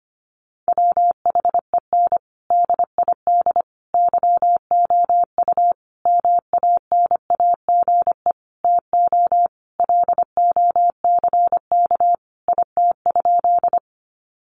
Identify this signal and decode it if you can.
Morse code, heard as a keyed tone, about 25 wpm, 700 Hz. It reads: WHEN DID YOU MANAGE TO LOCK IT?
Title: Sentences from Top 1400 Words 25wpm